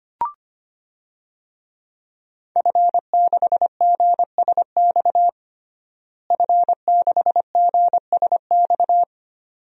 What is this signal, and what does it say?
Morse code, heard as a keyed tone, about 25 wpm, 700 Hz. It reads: F6GSX F6GSX
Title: DXCC-100 Call Signs - 5 Characters - Repeated 2x - 25wpm